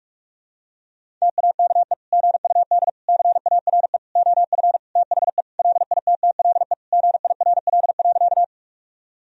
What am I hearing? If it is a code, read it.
Morse code, heard as a keyed tone, about 42 wpm, 700 Hz. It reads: TAKE GUD CARE OF THE LITTLE GIRL.
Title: Sentences from Top 400 Words and Abbreviations 45wpm